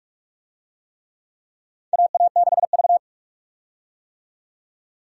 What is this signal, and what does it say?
Morse code, heard as a keyed tone, about 44 wpm, 700 Hz. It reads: AA6V
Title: USA Call Signs - 4 Characters - 45wpm